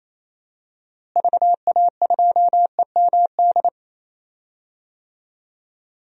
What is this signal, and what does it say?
Morse code, heard as a keyed tone, about 28 wpm, 700 Hz. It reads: VA2EMD
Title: DXCC-100 Call Signs - 6 Characters - Repeated 3x - 28wpm